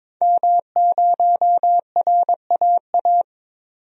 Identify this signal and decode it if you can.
Morse code, heard as a keyed tone, about 22 wpm, 700 Hz. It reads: M0RAA
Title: DXCC-100 Call Signs - 5 Characters - 22wpm